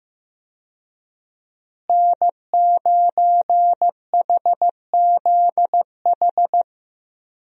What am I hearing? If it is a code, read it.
Morse code, heard as a keyed tone, about 16 wpm, 700 Hz. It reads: N9HZH
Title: USA Call Signs - 5 Characters - Repeated 3x - 15wpm